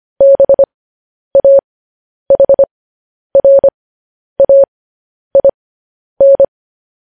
Morse code, heard as a keyed tone, about 25 wpm, spaced as for 10 wpm, 550 Hz. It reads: BAHRAIN